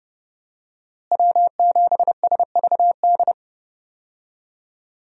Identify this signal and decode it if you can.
Morse code, heard as a keyed tone, about 32 wpm, 700 Hz. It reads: W7SVD
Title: USA Call Signs - 5 Characters - Repeated 2x - 30wpm